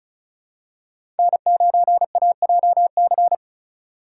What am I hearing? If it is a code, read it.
Morse code, heard as a keyed tone, about 36 wpm, 700 Hz. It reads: N9AJC
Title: USA Call Signs - 5 Characters - 2x Character Spacing - 35wpm